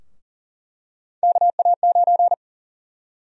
Morse code, heard as a keyed tone, about 40 wpm, 700 Hz. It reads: KA9
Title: USA Call Sign Prefixes - 40wpm